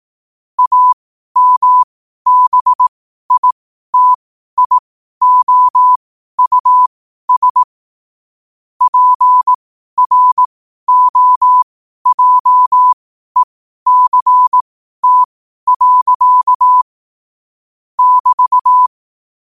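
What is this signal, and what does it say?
Morse code, heard as a keyed tone, about 18 wpm, spaced as for 12 wpm, 1000 Hz. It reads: AMBITIOUS PROJECT. <BT>
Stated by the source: Morse sent at 12 wpm.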